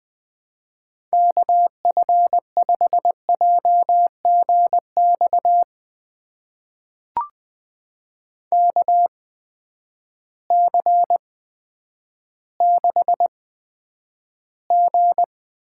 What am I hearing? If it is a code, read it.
Morse code, heard as a keyed tone, about 20 wpm, spaced as for 10 wpm, 700 Hz. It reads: KF5JGX KC6G
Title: USA Call Signs - 6 Characters - 8x Character Spacing - 20wpm